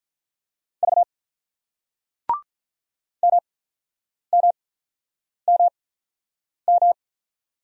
Morse code, heard as a keyed tone, about 48 wpm, 700 Hz. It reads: V M M M M